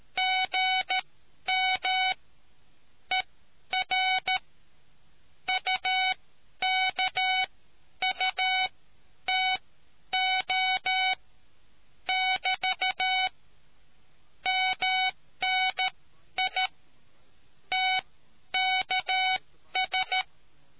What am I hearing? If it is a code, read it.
Morse code, heard as a keyed tone, about 13 wpm, spaced as for 10 wpm, 750 Hz. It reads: GM ER UKUTO <BT> MNI TKS